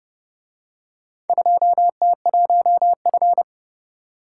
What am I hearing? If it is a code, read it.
Morse code, heard as a keyed tone, about 26 wpm, 700 Hz. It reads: 2T1F